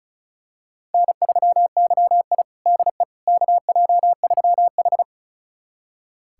Morse code, heard as a keyed tone, about 36 wpm, 700 Hz. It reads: N3YI DE KJ3H
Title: USA Callsign DE Callsign - 4 Characters - 35wpm